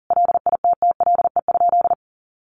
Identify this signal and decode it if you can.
Morse code, heard as a keyed tone, about 40 wpm, 700 Hz. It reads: LITTLE?
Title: Sentences from Top 900 Words 40wpm